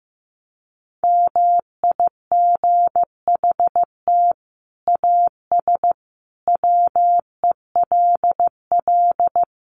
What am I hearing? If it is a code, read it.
Morse code, heard as a keyed tone, about 15 wpm, 700 Hz. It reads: MIGHT AS WELL
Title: Sets of 5 Words Encore2 15wpm